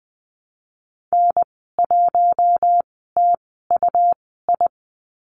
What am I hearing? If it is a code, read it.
Morse code, heard as a keyed tone, about 20 wpm, spaced as for 14 wpm, 700 Hz. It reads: N1TUI
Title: 179 Lesson 29g - Slash - Call Signs 20wpm